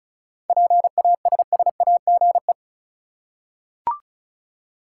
Morse code, heard as a keyed tone, about 34 wpm, 700 Hz. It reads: PASSAGE
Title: Seven Letter Words - 35wpm